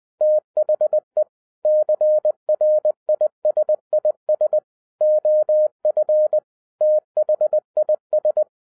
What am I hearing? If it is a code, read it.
Morse code, heard as a keyed tone, about 20 wpm, 610 Hz. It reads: THE CRISIS OF THIS